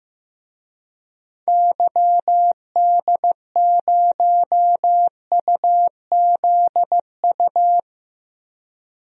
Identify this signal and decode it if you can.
Morse code, heard as a keyed tone, about 15 wpm, 700 Hz. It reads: YD0UZU
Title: DXCC-100 Call Signs - 6 Characters - Repeated 3x - 15wpm